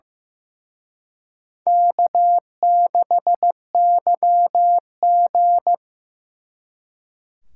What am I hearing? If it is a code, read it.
Morse code, heard as a keyed tone, about 18 wpm, 700 Hz. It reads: K6YG
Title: Speed Racing - USA 1x2 Call Signs 15wpm